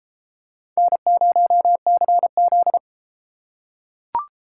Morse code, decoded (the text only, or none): N0CZ